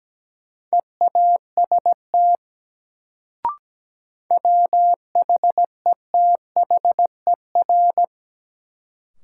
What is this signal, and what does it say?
Morse code, heard as a keyed tone, about 17 wpm, 700 Hz. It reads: EAST WHETHER
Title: Sentences from Top 1000 Words - Review Part 3 - New Word Drill - 17wpm